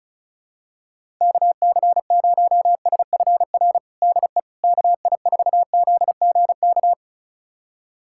Morse code, heard as a keyed tone, about 36 wpm, 700 Hz. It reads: KC0SFR DE KI4ZGK
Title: USA Callsign DE Callsign - 6 Characters - 35wpm